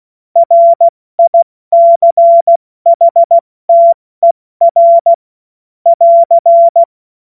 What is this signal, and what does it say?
Morse code, heard as a keyed tone, about 16 wpm, spaced as for 14 wpm, 680 Hz. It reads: RICHTER <AR>
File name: Morsekurs / Witze / Witz 30. August 2022 70 F80